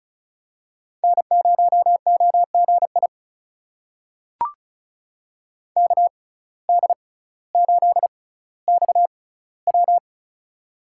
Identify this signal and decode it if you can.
Morse code, heard as a keyed tone, about 34 wpm, spaced as for 18 wpm, 700 Hz. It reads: N0OGI KD8XW
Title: USA Call Signs - 5 Characters - 6x Character Spacing - 35wpm